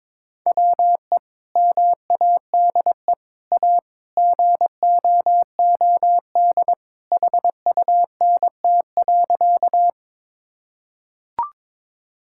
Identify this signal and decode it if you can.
Morse code, heard as a keyed tone, about 22 wpm, 700 Hz. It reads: WE MADE A GOOD HUNT.
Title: Sentences from Top 1600 Words 22wpm